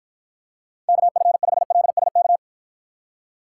Morse code, heard as a keyed tone, about 52 wpm, 700 Hz. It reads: KF5LSK